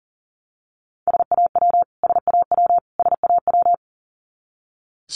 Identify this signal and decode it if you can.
Morse code, heard as a keyed tone, about 40 wpm, 700 Hz. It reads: SAW SAW SAW
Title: Top 300 Words Repeated 40wpm